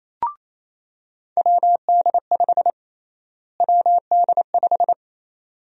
Morse code, heard as a keyed tone, about 28 wpm, 700 Hz. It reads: WD5 WD5
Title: USA Call Sign Prefixes - Repeated 2x - 28wpm